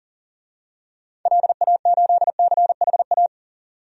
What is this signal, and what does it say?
Morse code, heard as a keyed tone, about 40 wpm, 700 Hz. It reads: LA8CHA
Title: DXCC-100 Call Signs - 6 Characters - 40wpm